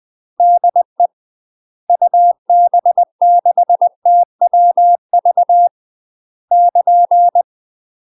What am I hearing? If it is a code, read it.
Morse code, heard as a keyed tone, about 20 wpm, 700 Hz. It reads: DE UB6TWV <KN>
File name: UK MASTER QSOs TRAINING / 20WPM / UK MASTER QSOs 0046